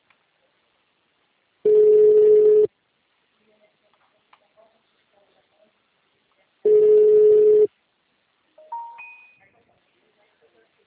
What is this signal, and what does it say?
Ringback tone (Europe, 425 Hz), with speech.